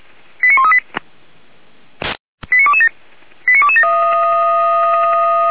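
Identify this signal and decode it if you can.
Five-tone selective call (ZVEI), heard as quick runs of five tones, tones of about 70 ms. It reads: 89138 89227 89227
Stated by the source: ZVEI five-tone selective call received by SDR, from the Feuerwehr Orlamünde audio archive.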